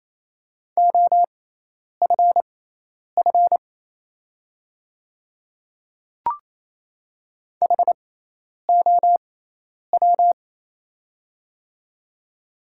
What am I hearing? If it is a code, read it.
Morse code, heard as a keyed tone, about 27 wpm, spaced as for 10 wpm, 690 Hz. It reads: OFF HOW
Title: 6x Character Spacing - Top 300 Words - 28wpm